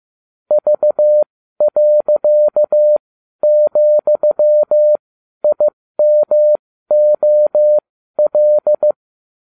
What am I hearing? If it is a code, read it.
Morse code, heard as a keyed tone, about 15 wpm, spaced as for 12 wpm, 600 Hz. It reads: V.,IMOL